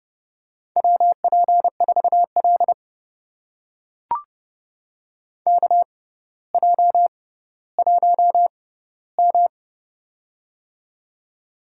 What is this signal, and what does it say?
Morse code, heard as a keyed tone, about 30 wpm, spaced as for 15 wpm, 700 Hz. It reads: WP4L KJ1M
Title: USA Call Signs - 4 Characters - 6x Character Spacing - 30wpm